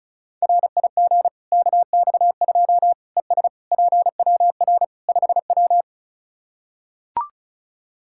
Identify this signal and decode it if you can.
Morse code, heard as a keyed tone, about 36 wpm, 700 Hz. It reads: RIG KX2 ES PWR 5W
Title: K1USN SST QSO Practice 35wpm